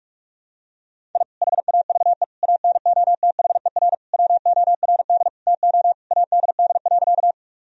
Morse code, heard as a keyed tone, about 43 wpm, 700 Hz. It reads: I HAVE ANOTHER WORD TO ADD.
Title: Sentences from Top 1000 Words - Review Part 1 - 45wpm